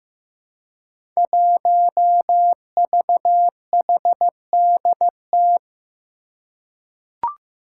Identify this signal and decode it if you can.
Morse code, heard as a keyed tone, about 15 wpm, 700 Hz. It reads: 1VHDT
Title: Five Challenging Letter-Number Combinations 15wpm